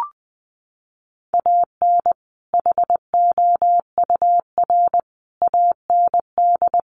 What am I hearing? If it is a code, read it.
Morse code, heard as a keyed tone, about 20 wpm, 700 Hz. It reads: AN HOUR AND
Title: Sentences from Top 400 Words 20wpm